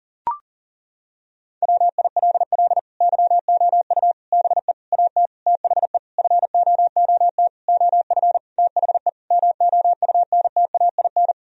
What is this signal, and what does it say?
Morse code, heard as a keyed tone, about 37 wpm, 700 Hz. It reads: WILL YOU BE AT THE FOOT OF THE MOUNTAIN
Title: Sentences from Top 1000 Words - Review Part 3 - 40wpm